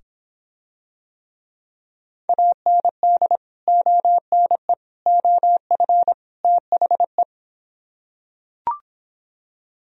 Morse code, heard as a keyed tone, about 26 wpm, 700 Hz. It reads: AND ONE OF THE